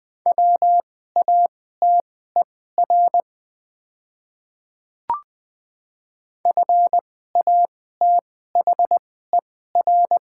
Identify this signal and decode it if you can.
Morse code, heard as a keyed tone, about 20 wpm, spaced as for 15 wpm, 700 Hz. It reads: WATER FATHER
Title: 2x Character Spacing - Top 500 Words - 20wpm